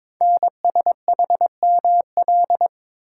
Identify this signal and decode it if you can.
Morse code, heard as a keyed tone, about 22 wpm, 700 Hz. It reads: NSHML
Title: Code Groups - 5 Letters - 22wpm